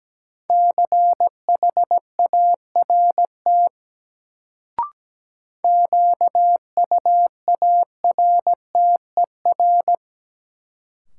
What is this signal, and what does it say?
Morse code, heard as a keyed tone, about 17 wpm, 700 Hz. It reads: CHART QUARTER
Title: Sentences from Top 1300 Words - New Word Drill Alt - 17wpm